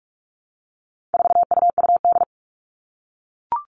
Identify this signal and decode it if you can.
Morse code, heard as a keyed tone, about 44 wpm, 700 Hz. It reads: 4UUD
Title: All Cut Numbers - 4-Digit 45wpm